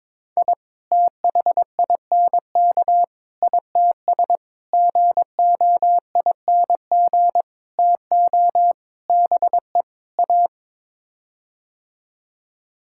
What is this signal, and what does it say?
Morse code, heard as a keyed tone, about 22 wpm, 700 Hz. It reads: I THINK ITS GOING TO BE A
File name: practice / 22wpm / Sets of 7 Words - 22wpm